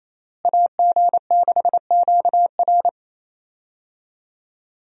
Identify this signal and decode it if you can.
Morse code, heard as a keyed tone, about 28 wpm, 700 Hz. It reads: AG6QR
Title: USA Call Signs - 5 Characters - Repeated 3x - 28wpm